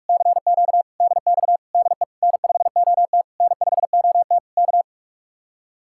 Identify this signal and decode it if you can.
Morse code, heard as a keyed tone, about 59 wpm, 700 Hz. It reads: CQ DX DE N5OT N5OT K